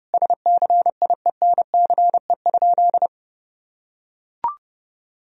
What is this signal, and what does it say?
Morse code, heard as a keyed tone, about 32 wpm, 700 Hz. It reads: SCIENCE?
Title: Sentences from Top 1000 Words - Review Part 3 - 30wpm